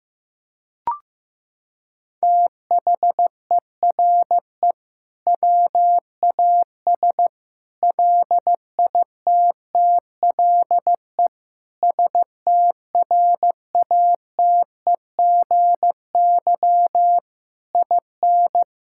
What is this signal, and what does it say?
Morse code, heard as a keyed tone, about 15 wpm, 700 Hz. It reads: THERE WAS LITTLE STRATEGY IN
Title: Sentences from Top 1000 Words - Review Part 5 - 15wpm